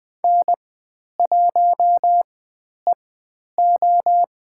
Morse code, heard as a keyed tone, about 20 wpm, spaced as for 10 wpm, 700 Hz. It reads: N1EO
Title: 109 Lesson 18g - Letter m - Call Signs 20wpm